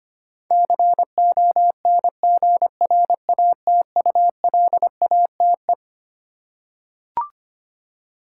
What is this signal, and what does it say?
Morse code, heard as a keyed tone, about 25 wpm, 700 Hz. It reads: CONGRATULATE
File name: Twelve Letter Words - 25wpm